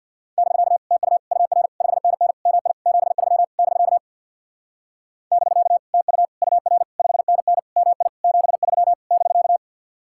Morse code, heard as a keyed tone, about 66 wpm, 700 Hz. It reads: <BK> TU UR 5NN MI 73 <BK> <BK> TU UR 5NN MI 73 <BK>